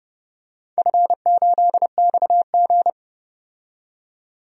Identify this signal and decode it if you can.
Morse code, heard as a keyed tone, about 31 wpm, 700 Hz. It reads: F8XG
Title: DXCC-100 Call Signs - 4 Characters - 30wpm